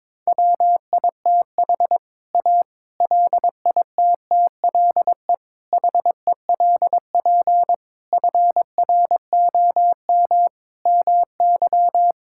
Morse code, heard as a keyed tone, about 22 wpm, 700 Hz. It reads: WITH A LITTLE HELP FROM MY
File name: practice / 22wpm / Sets of 7 Words - 22wpm